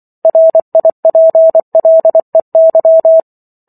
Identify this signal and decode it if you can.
Morse code, heard as a keyed tone, about 24 wpm, 650 Hz. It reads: RIPLEY